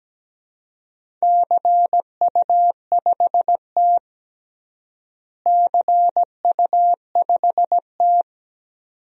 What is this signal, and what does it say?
Morse code, heard as a keyed tone, about 17 wpm, 700 Hz. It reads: CU5T CU5T
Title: DXCC-100 Call Signs - 4 Characters - Repeated 2x - 17wpm